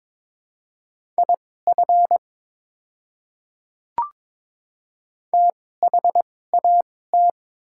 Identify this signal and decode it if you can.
Morse code, heard as a keyed tone, about 22 wpm, spaced as for 16 wpm, 690 Hz. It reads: IF THAT
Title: 2x Character Spacing - Top 100 Words - 22wpm